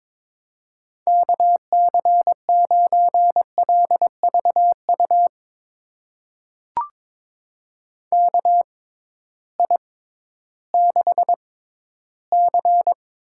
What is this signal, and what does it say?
Morse code, heard as a keyed tone, about 22 wpm, spaced as for 13 wpm, 700 Hz. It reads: KC9LVU KI6C
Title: USA Call Signs - 6 Characters - 6x Character Spacing - 22wpm